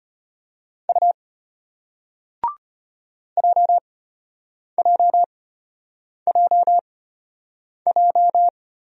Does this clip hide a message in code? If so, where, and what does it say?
Yes: Morse code from 0.9 to 8.5 s, heard as a keyed tone, about 32 wpm, spaced as for 8 wpm, 700 Hz. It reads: U JJJJ